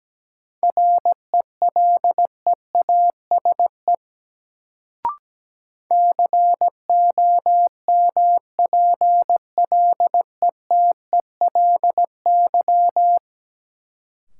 Morse code, heard as a keyed tone, about 17 wpm, 700 Hz. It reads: RELEASE COMPLETELY
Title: Sentences from Top 1100 Words - New Word Drill - 17wpm